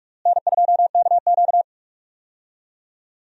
Morse code, heard as a keyed tone, about 43 wpm, 700 Hz. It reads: N2KQ
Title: USA Call Signs - 4 Characters - Repeated 2x - 45wpm